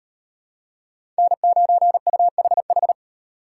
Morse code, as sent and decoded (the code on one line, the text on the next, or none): -. ----. ..- .... ....
N9UHH